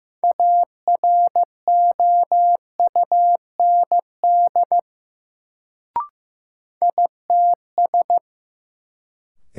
Morse code, heard as a keyed tone, about 15 wpm, 700 Hz. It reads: AROUND ITS
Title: Sentences from Top 600 Words - New Word Drill - 15wpm